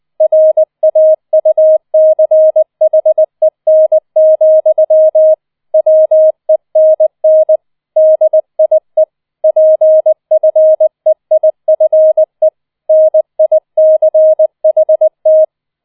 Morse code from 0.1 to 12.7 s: RAUCHEN, WENN DIE PFEIFE